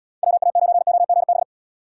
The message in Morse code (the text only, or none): WA9OGB